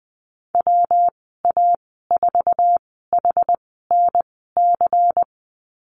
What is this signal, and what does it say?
Morse code, heard as a keyed tone, about 20 wpm, spaced as for 15 wpm, 700 Hz. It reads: WA4HNC
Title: 231 Lesson 37g - Letter x - Call Signs 20wpm